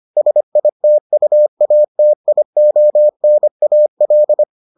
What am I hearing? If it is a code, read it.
Morse code, heard as a keyed tone, about 25 wpm, 600 Hz. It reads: SITUATIONAL